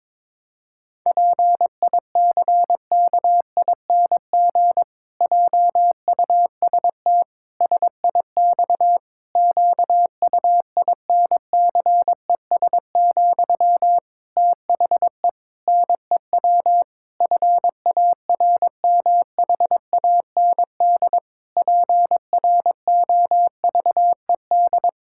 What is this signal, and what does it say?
Morse code, heard as a keyed tone, about 22 wpm, 700 Hz. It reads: PICKING JUST SIX QUINCES, THE NEW FARMHAND PROVED
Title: Pangram Challenge v2 - 22wpm